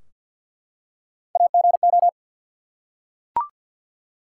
Morse code, heard as a keyed tone, about 53 wpm, 700 Hz. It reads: AGO
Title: Three Letter Words - 50wpm